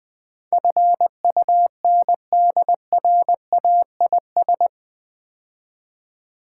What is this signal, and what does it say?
Morse code, heard as a keyed tone, about 20 wpm, 700 Hz. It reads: FUNDRAIS
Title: Build A Word - 4 Letters At A Time - 11-Letter Words - 20wpm